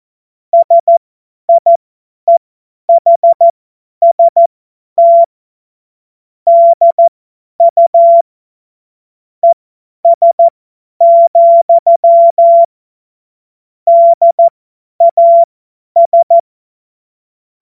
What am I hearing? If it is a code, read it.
Morse code, heard as a keyed tone, about 14 wpm, spaced as for 10 wpm, 680 Hz. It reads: SIEHST DU ES, DAS
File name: Morsekurs / Witze / Witz 31.Oktober 2023 50 F70